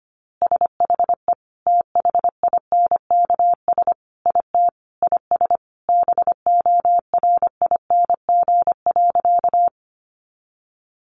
Morse code, heard as a keyed tone, about 25 wpm, 700 Hz. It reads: SHE THINKS IT IS BORING.
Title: Easy Phrases 25wpm